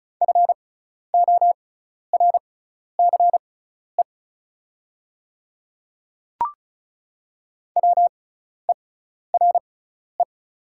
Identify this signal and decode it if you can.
Morse code, heard as a keyed tone, about 36 wpm, spaced as for 12 wpm, 700 Hz. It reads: FORCE WERE
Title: 6x Character Spacing - Top 500 Words - 35wpm